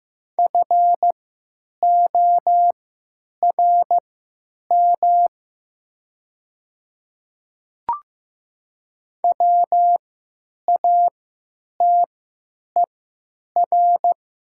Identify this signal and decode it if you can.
Morse code, heard as a keyed tone, about 15 wpm, spaced as for 8 wpm, 700 Hz. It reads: FORM WATER
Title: 3x Character Spacing - Top 500 Words - 15wpm